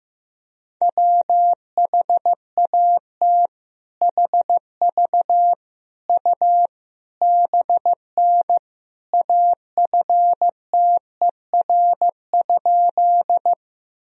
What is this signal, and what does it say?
Morse code, heard as a keyed tone, about 15 wpm, 700 Hz. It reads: WHAT HV U BN AFTER?
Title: Sentences from Top 200 Words and Abbreviations 15wpm